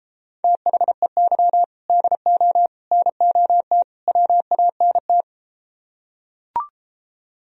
Morse code, heard as a keyed tone, about 33 wpm, 700 Hz. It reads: THEY DO NOT WANT